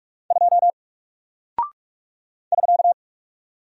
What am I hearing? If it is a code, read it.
Morse code, heard as a keyed tone, about 46 wpm, 700 Hz. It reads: 2 <SK>
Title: Single Character (Letter:Number:Punctuation:Prosign) Alt 45wpm